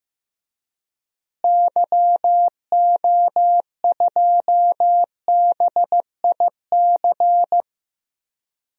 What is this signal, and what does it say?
Morse code, heard as a keyed tone, about 15 wpm, 700 Hz. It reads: YO2BIC
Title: DXCC-100 Call Signs - 6 Characters - Repeated 3x - 15wpm